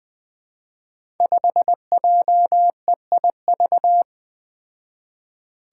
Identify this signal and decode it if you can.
Morse code, heard as a keyed tone, about 20 wpm, 700 Hz. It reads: 5JEIV
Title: Five Challenging Letter-Number Combinations 20wpm